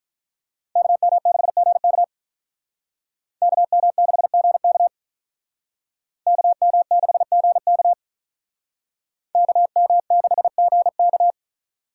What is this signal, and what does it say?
Morse code, heard as a keyed tone, about 44 wpm, 700 Hz. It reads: KM6GK KM6GK KM6GK KM6GK